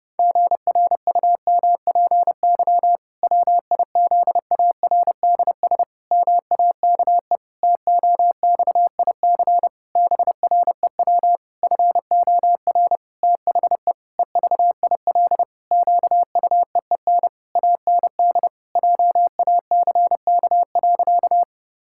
Morse code, heard as a keyed tone, about 30 wpm, 700 Hz. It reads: GRUMPY WIZARDS MAKE TOXIC BREW FOR THE EVIL QUEEN AND JACK.
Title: Pangram Challenge v2 - 30wpm